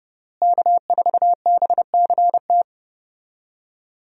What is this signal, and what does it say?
Morse code, heard as a keyed tone, about 32 wpm, 700 Hz. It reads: K4BCT
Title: USA Call Signs - 5 Characters - Repeated 3x - 30wpm